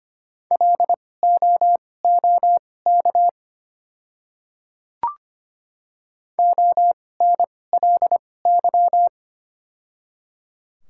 Morse code, heard as a keyed tone, about 25 wpm, spaced as for 18 wpm, 700 Hz. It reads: LOOK ONLY
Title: 2x Character Spacing - Top 200 Words - 25wpm